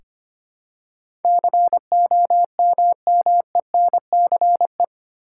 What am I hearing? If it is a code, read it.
Morse code, heard as a keyed tone, about 25 wpm, 700 Hz. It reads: COMMENCE
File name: Eight Letter Words - 25wpm